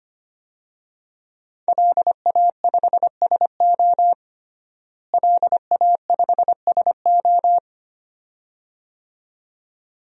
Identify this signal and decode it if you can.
Morse code, heard as a keyed tone, about 25 wpm, 700 Hz. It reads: LA5SO LA5SO
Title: DXCC-100 Call Signs - 5 Characters - Repeated 2x - 25wpm